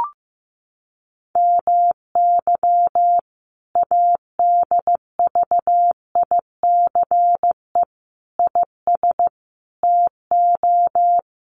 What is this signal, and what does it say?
Morse code, heard as a keyed tone, about 14 wpm, 700 Hz. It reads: MY ADVICE IS TO
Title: Sentences from Top 1300 Words 15wpm